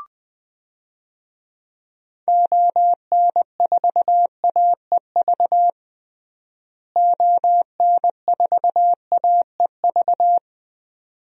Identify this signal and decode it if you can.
Morse code, heard as a keyed tone, about 20 wpm, 700 Hz. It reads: ON4AEV ON4AEV
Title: DXCC-100 Call Signs - 6 Characters - Repeated 3x - 20wpm